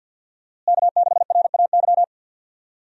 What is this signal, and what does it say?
Morse code, heard as a keyed tone, about 51 wpm, 700 Hz. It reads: K6RAY
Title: USA Call Signs - 5 Characters - Repeated 2x - 50wpm